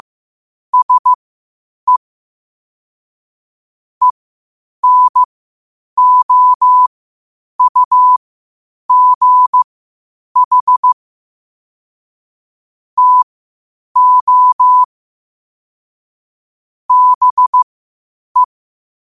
Morse code, heard as a keyed tone, about 15 wpm, spaced as for 8 wpm, 1000 Hz. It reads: SE ENOUGH TO BE
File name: qotdfiles / 8wpm